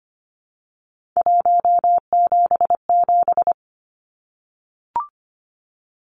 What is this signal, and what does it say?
Morse code, heard as a keyed tone, about 25 wpm, 700 Hz. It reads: 177
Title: Numbers - 3-Digit 25wpm